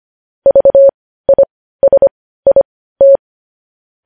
Morse code, heard as a keyed tone, about 25 wpm, spaced as for 15 wpm, 550 Hz. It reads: VISIT
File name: podcast / News headlines-1x-25x15